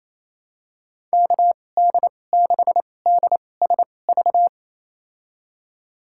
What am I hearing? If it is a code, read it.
Morse code, heard as a keyed tone, about 28 wpm, spaced as for 20 wpm, 700 Hz. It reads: KD6DSV
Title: USA Call Signs - 6 Characters - 2x Character Spacing - 28wpm